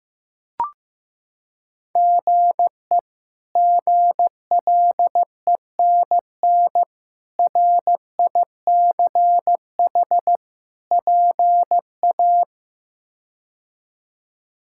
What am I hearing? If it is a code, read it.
Morse code, heard as a keyed tone, about 15 wpm, 700 Hz. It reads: GE GLENN RICH PA
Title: K1USN SST QSO Practice - 15wpm